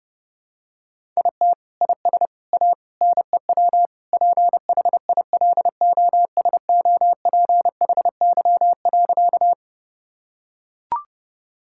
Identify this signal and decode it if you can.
Morse code, heard as a keyed tone, about 30 wpm, 700 Hz. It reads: IT IS A NEW PHILOSOPHY.
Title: Sentences from Top 1000 Words - Review Part 6 - 30wpm